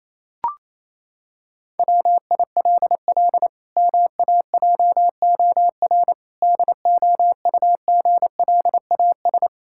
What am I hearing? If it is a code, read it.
Morse code, heard as a keyed tone, about 28 wpm, 700 Hz. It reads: WILL MAJOR DOUGLAS
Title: Pangram Challenge v2 - 28wpm